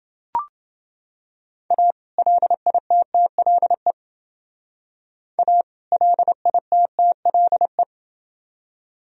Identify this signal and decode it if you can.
Morse code, heard as a keyed tone, about 28 wpm, 700 Hz. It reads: A LITTLE A LITTLE